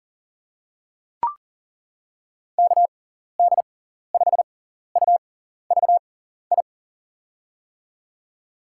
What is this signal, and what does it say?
Morse code, heard as a keyed tone, about 41 wpm, spaced as for 14 wpm, 700 Hz. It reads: KD5UVI
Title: USA Call Signs - 6 Characters - 6x Character Spacing - 40wpm